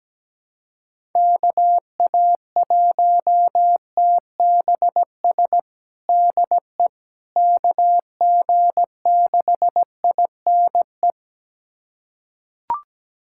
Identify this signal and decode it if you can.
Morse code, heard as a keyed tone, about 17 wpm, 700 Hz. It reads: KA1TBS DE KG6INE
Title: USA Callsign DE Callsign 17wpm